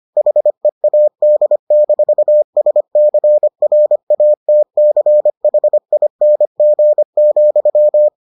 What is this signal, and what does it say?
Morse code, heard as a keyed tone, about 25 wpm, 600 Hz. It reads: HEAD-SCRATCHING,